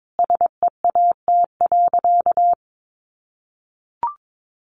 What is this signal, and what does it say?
Morse code, heard as a keyed tone, about 22 wpm, 700 Hz. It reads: SEAT.
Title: Sentences from Top 1500 Words 22wpm